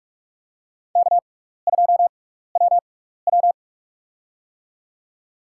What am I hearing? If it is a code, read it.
Morse code, heard as a keyed tone, about 44 wpm, spaced as for 15 wpm, 700 Hz. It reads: K2WW
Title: USA Call Signs - 4 Characters - 6x Character Spacing - 45wpm